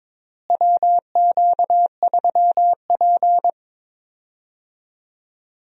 Morse code, heard as a keyed tone, about 22 wpm, 700 Hz. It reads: WQ3P
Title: USA Call Signs - 4 Characters - Repeated 3x - 22wpm